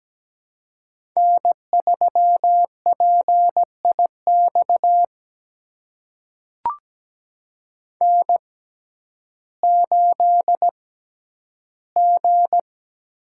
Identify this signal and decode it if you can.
Morse code, heard as a keyed tone, about 17 wpm, spaced as for 10 wpm, 700 Hz. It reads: N3PIX N8G
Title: USA Call Signs - 5 Characters - 6x Character Spacing - 17wpm